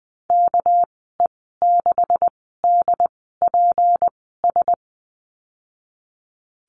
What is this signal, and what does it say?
Morse code, heard as a keyed tone, about 19 wpm, spaced as for 14 wpm, 700 Hz. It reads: KE6DPS
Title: 245 Lesson 39g - Letter z - Call Signs 20wpm